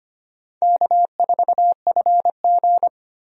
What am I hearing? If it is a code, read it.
Morse code, heard as a keyed tone, about 25 wpm, 700 Hz. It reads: K4FG
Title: USA Call Signs - 4 Characters - 25wpm